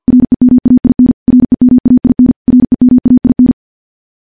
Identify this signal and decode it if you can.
Morse code, heard as a keyed tone, about 46 wpm, 250 Hz. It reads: REMAIN REMAIN REMAIN
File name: morse / morse-training-50WPM